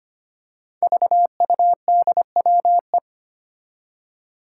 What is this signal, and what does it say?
Morse code, heard as a keyed tone, about 25 wpm, 700 Hz. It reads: VUDWE
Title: Five Challenging Letter-Number Combinations 25wpm